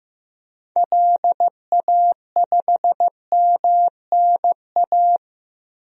Morse code, heard as a keyed tone, about 16 wpm, 700 Hz. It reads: LA5MNA
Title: DXCC-100 Call Signs - 6 Characters - Repeated 2x - 15wpm